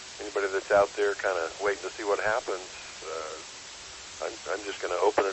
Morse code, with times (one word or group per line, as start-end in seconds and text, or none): none